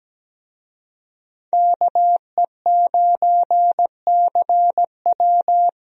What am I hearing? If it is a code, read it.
Morse code, heard as a keyed tone, about 17 wpm, 700 Hz. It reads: KE9CW
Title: USA Call Signs - 5 Characters - Repeated 3x - 17wpm